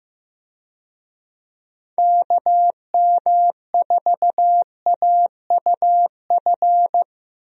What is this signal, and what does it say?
Morse code, heard as a keyed tone, about 14 wpm, 700 Hz. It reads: KM4AUF
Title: USA Call Signs - 6 Characters - Repeated 3x - 15wpm